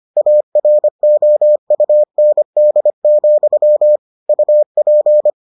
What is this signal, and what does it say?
Morse code, heard as a keyed tone, about 25 wpm, 600 Hz. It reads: AROUND, UP